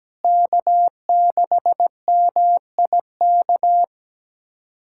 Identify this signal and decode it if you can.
Morse code, heard as a keyed tone, about 17 wpm, 700 Hz. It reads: K6MIK
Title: USA Callsign DE Callsign - 5 Characters - 17wpm